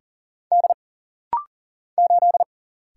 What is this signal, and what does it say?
Morse code, heard as a keyed tone, about 40 wpm, 700 Hz. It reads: D 8
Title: Single Letter-Number Alt - Rapid-Fire 40wpm